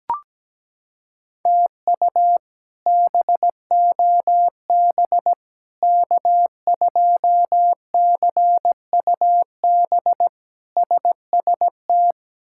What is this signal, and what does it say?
Morse code, heard as a keyed tone, about 17 wpm, 700 Hz. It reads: TU BOB K2CUB SST
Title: K1USN SST QSO Practice - 17wpm